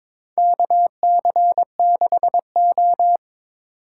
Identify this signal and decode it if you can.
Morse code, heard as a keyed tone, about 21 wpm, 700 Hz. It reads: KC6O
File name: practice / 22wpm / Four Letter-Number Combinations - 22wpm